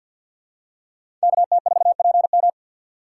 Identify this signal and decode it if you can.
Morse code, heard as a keyed tone, about 46 wpm, 700 Hz. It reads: KT4PM
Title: USA Call Signs - 5 Characters - 4x Character Spacing - 50wpm